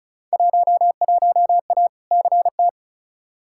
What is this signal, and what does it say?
Morse code, heard as a keyed tone, about 36 wpm, 700 Hz. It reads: 11A CT
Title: Field Day Exchanges Encore - 35wpm